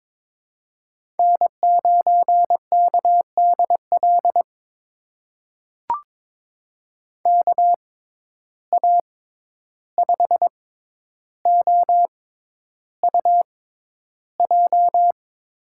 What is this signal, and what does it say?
Morse code, heard as a keyed tone, about 21 wpm, spaced as for 11 wpm, 700 Hz. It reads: N9KDL KA5OUJ
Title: USA Call Signs - 6x Character Spacing - 22wpm